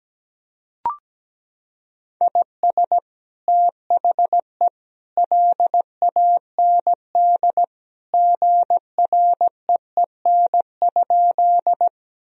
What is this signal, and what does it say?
Morse code, heard as a keyed tone, about 17 wpm, 700 Hz. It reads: IS THE LAND GREEN?
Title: Sentences from Top 1000 Words - Review Part 2 - 17wpm